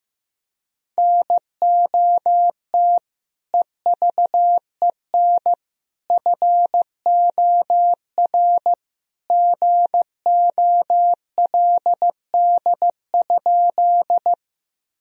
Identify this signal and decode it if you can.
Morse code, heard as a keyed tone, about 15 wpm, 700 Hz. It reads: NOT EVEN FOR GOLD?
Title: Sentences from Top 1000 Words - Review Part 3 - 15wpm